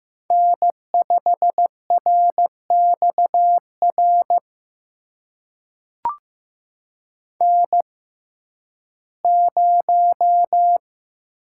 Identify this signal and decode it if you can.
Morse code, heard as a keyed tone, about 15 wpm, spaced as for 11 wpm, 700 Hz. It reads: N5RXR N0
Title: USA Call Signs - 5 Characters - 6x Character Spacing - 15wpm